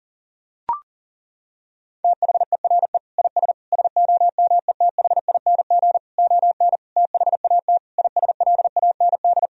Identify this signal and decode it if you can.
Morse code, heard as a keyed tone, about 40 wpm, 700 Hz. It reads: THERE IS SOMETHING ON THAT ISLAND
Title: Sentences from Top 1000 Words - Review Part 3 - 40wpm